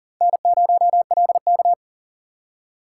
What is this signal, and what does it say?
Morse code, heard as a keyed tone, about 39 wpm, 700 Hz. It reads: N0LK
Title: USA Call Signs - 4 Characters - 2x Character Spacing - 40wpm